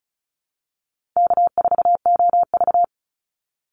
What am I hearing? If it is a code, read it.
Morse code, heard as a keyed tone, about 34 wpm, 700 Hz. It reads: K4OV
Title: Call Signs Repeated 2x - 35wpm